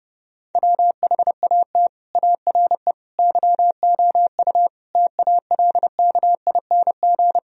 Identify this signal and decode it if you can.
Morse code, heard as a keyed tone, about 33 wpm, 700 Hz. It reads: WHAT ARE YOU TALKING